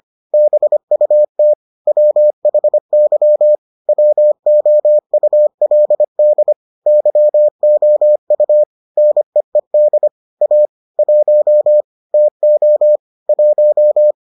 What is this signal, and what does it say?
Morse code, heard as a keyed tone, about 25 wpm, 600 Hz. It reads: BUT WHY WOULD YOU NEED A 1 TO 1